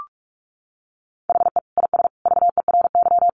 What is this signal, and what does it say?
Morse code, heard as a keyed tone, about 44 wpm, 700 Hz. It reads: HE IS VERY
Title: Easy Phrases 45wpm